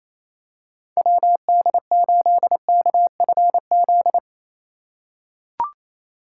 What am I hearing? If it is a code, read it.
Morse code, heard as a keyed tone, about 28 wpm, 700 Hz. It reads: WD8KFZ
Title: USA Call Signs - 3x Character Spacing - 28wpm